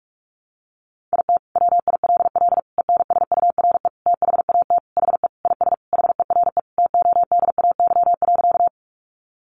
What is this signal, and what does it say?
Morse code, heard as a keyed tone, about 44 wpm, 700 Hz. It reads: IT WILL ENSURE THAT HE IS HERE TODAY.
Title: Sentences from Top 1100 Words 45wpm